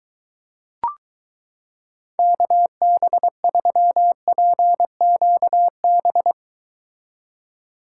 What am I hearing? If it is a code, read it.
Morse code, heard as a keyed tone, about 23 wpm, 700 Hz. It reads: KB3PQB